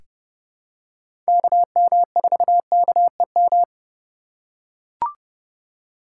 Morse code, heard as a keyed tone, about 29 wpm, 700 Hz. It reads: KM4KEM
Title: USA Call Signs - 8x Character Spacing - 30wpm